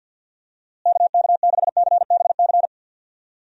Morse code, heard as a keyed tone, about 51 wpm, 700 Hz. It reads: KK6CBC
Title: USA Call Signs - 6 Characters - Repeated 3x - 50wpm